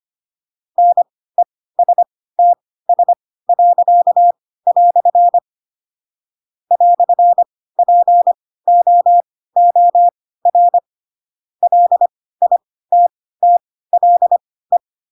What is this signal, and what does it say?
Morse code, heard as a keyed tone, about 25 wpm, spaced as for 16 wpm, 700 Hz. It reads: NESTS." "POOR LITTLE
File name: Pinocchio / Pinocchio ch4 10 25 ik1hge